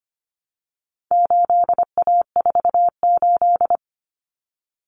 Morse code, heard as a keyed tone, about 25 wpm, 700 Hz. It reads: 8A48
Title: Common Cut Numbers - 4-Digit 25wpm